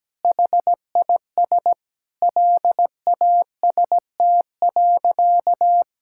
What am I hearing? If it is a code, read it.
Morse code, heard as a keyed tone, about 17 wpm, 700 Hz. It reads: HIS LAST.
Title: Sentences from Top 1600 Words 17wpm